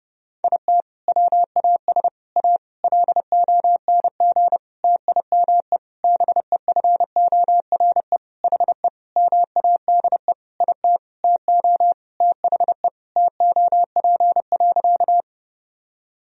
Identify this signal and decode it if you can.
Morse code, heard as a keyed tone, about 32 wpm, 700 Hz. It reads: IT WAS A LONG TIME BEFORE HE MADE IT TO THE TOP.
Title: Sentences from Top 1000 Words - Review Part 2 - 30wpm